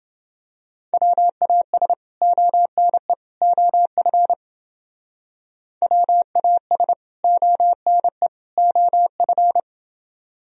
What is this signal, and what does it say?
Morse code, heard as a keyed tone, about 30 wpm, 700 Hz. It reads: WAS ONE OF WAS ONE OF